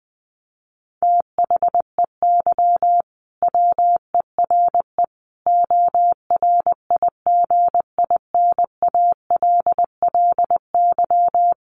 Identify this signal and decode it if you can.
Morse code, heard as a keyed tone, about 19 wpm, 700 Hz. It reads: THEY WERE ORIGINALLY
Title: Sentences from Top 1500 Words 20wpm